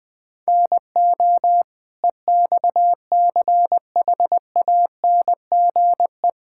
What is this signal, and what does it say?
Morse code, heard as a keyed tone, about 20 wpm, 700 Hz. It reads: NO EXCHANGE
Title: Sentences from Top 1000 Words - Review Part 6 - 20wpm